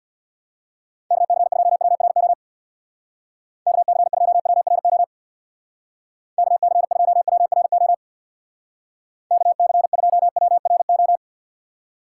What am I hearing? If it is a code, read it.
Morse code, heard as a keyed tone, about 61 wpm, 700 Hz. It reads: KC2WRO KC2WRO KC2WRO KC2WRO
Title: Speed Racing - USA 2x3 Call Signs 50wpm